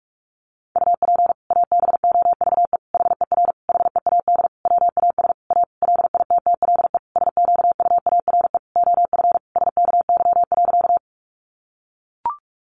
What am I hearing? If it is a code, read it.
Morse code, heard as a keyed tone, about 44 wpm, 700 Hz. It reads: UP ABOVE HER HEAD WAS A LITTLE SQUARE OF SKY.
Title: Sentences from Top 1200 Words 45wpm